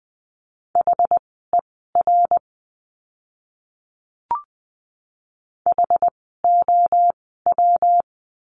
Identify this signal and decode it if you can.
Morse code, heard as a keyed tone, about 20 wpm, spaced as for 14 wpm, 700 Hz. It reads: HER HOW
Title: 192 Lesson 31f - Letter v - Words 20wpm